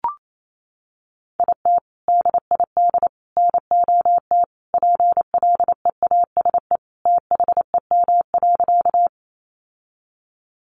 Morse code, heard as a keyed tone, about 28 wpm, 700 Hz. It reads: IT DID NOT PLEASE THEM.
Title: Sentences from Top 1500 Words 28wpm